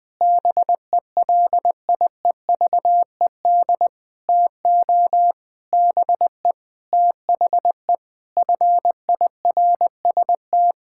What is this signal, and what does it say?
Morse code, heard as a keyed tone, about 21 wpm, 700 Hz. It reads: BELIEVED TO BE THE FIRST
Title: Sentences with 6-Word Phrases - 20wpm